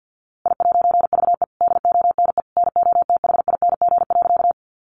Morse code, heard as a keyed tone, about 50 wpm, 700 Hz. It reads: I'VE DONE NOTHING.
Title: A Sound Of Thunder 50wpm